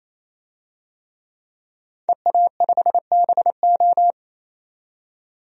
Morse code, heard as a keyed tone, about 28 wpm, 700 Hz. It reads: EA5BO
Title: DXCC-100 Call Signs - 5 Characters - 28wpm